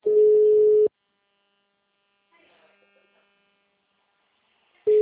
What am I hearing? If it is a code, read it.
Ringback tone (Europe, 425 Hz).